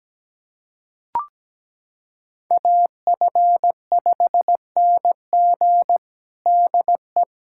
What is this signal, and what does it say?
Morse code, heard as a keyed tone, about 17 wpm, 700 Hz. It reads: AF5NG DE
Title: USA Callsign DE Callsign - 5 Characters - 17wpm